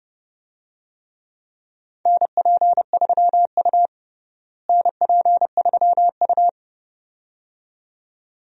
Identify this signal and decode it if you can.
Morse code, heard as a keyed tone, about 32 wpm, 700 Hz. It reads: NP3U NP3U
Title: USA Call Signs - 4 Characters - Repeated 2x - 30wpm